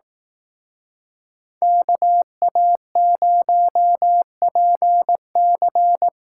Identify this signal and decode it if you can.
Morse code, heard as a keyed tone, about 18 wpm, 700 Hz. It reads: KA0PC
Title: Speed Racing - USA 2x2 Call Signs 15wpm